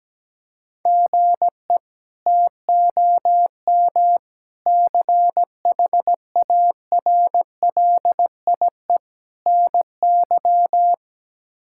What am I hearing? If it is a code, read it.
Morse code, heard as a keyed tone, about 17 wpm, 700 Hz. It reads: GE TOM CHARLIE NY
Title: K1USN SST QSO Practice - 17wpm